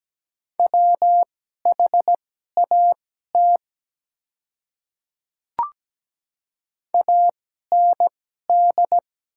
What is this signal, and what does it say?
Morse code, heard as a keyed tone, about 17 wpm, spaced as for 12 wpm, 700 Hz. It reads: WHAT AND
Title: 2x Character Spacing - Top 100 Words - 17wpm